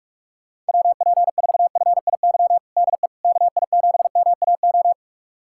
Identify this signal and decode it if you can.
Morse code, heard as a keyed tone, about 42 wpm, 700 Hz. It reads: WP4FIY DE KI7MAO
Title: USA Callsign DE Callsign 45wpm